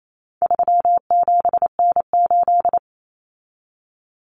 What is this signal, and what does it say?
Morse code, heard as a keyed tone, about 28 wpm, 700 Hz. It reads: 37N8
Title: Common Cut Numbers - 4-Digit 28wpm